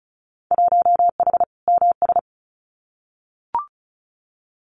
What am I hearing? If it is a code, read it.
Morse code, heard as a keyed tone, about 34 wpm, 700 Hz. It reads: 1H MS
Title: Winter Field Day Exchanges 35wpm